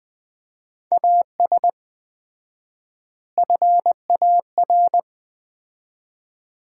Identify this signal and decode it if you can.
Morse code, heard as a keyed tone, about 21 wpm, 700 Hz. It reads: AS FAR
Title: 4x Word-Spacing - Sets of 3 Words - 20wpm